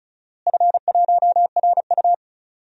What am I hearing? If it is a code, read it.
Morse code, heard as a keyed tone, about 34 wpm, 700 Hz. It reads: F1RU
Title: DXCC-100 Call Signs - 4 Characters - Repeated 3x - 35wpm